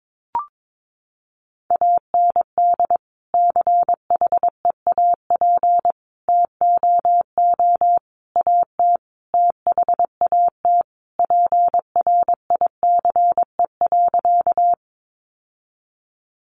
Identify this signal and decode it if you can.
Morse code, heard as a keyed tone, about 22 wpm, 700 Hz. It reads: AND CHEAP TOO AT THAT PRICE.
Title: Sentences from Top 1200 Words 22wpm